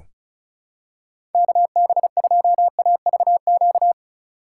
Morse code, heard as a keyed tone, about 34 wpm, 700 Hz. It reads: KB2AVQ
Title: USA Call Signs - 6 Characters - 2x Character Spacing - 35wpm